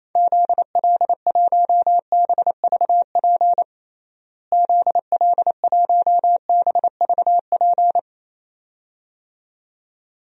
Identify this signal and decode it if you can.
Morse code, heard as a keyed tone, about 28 wpm, 700 Hz. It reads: ZL1BVP ZL1BVP
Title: DXCC-100 Call Signs - 6 Characters - Repeated 2x - 28wpm